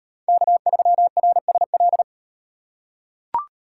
Morse code, heard as a keyed tone, about 36 wpm, 700 Hz. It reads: K3RSL